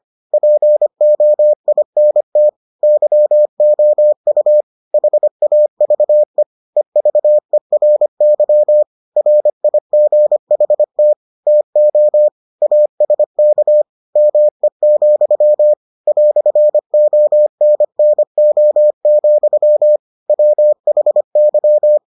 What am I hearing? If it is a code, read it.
Morse code, heard as a keyed tone, about 25 wpm, 600 Hz. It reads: POINT YOU HAVE EVERY RIGHT TO ASK ME, "ONNO, WHY